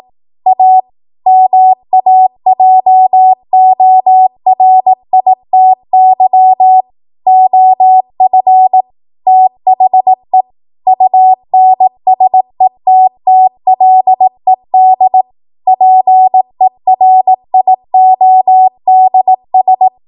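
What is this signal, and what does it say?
Morse code, heard as a keyed tone, about 18 wpm, 750 Hz. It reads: A MAJORITY OF THE UNSETTLED PERIODS
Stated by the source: ARRL 18 wpm Morse code practice archive transmission.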